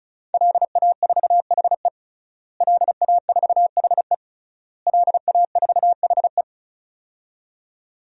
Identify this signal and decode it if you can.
Morse code, heard as a keyed tone, about 36 wpm, 700 Hz. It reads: LA4HE LA4HE LA4HE
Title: DXCC-100 Call Signs - 5 Characters - Repeated 3x - 35wpm